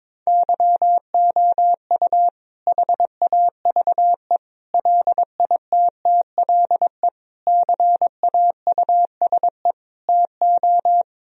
Morse code, heard as a keyed tone, about 22 wpm, 700 Hz. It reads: YOU HAVE LITTLE CAUSE TO
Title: Sentences from Top 1000 Words - Review Part 1 - 22wpm